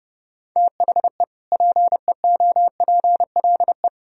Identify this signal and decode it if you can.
Morse code, heard as a keyed tone, about 30 wpm, 700 Hz. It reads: THE PEOPLE
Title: Sentences from Top 1000 Words - Review Part 3 - 30wpm